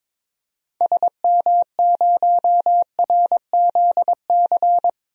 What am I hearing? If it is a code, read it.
Morse code, heard as a keyed tone, about 22 wpm, 700 Hz. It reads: SM0RZC
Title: DXCC-100 Call Signs - 6 Characters - Repeated 2x - 22wpm